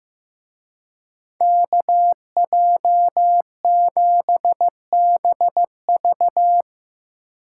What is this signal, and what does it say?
Morse code, heard as a keyed tone, about 16 wpm, 700 Hz. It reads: KJ7BV
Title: Speed Racing - USA 2x2 Call Signs 15wpm